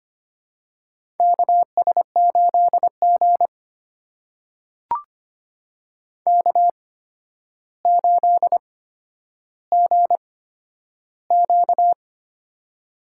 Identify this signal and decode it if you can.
Morse code, heard as a keyed tone, about 25 wpm, spaced as for 11 wpm, 700 Hz. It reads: KS8G K8GQ
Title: USA Call Signs - 4 Characters - 8x Character Spacing - 25wpm